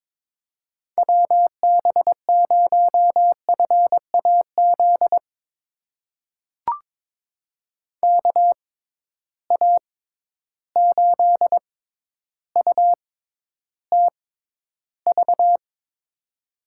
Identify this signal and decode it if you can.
Morse code, heard as a keyed tone, about 22 wpm, spaced as for 11 wpm, 700 Hz. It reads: WB0FAZ KA8UTV
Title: USA Call Signs - 6 Characters - 6x Character Spacing - 22wpm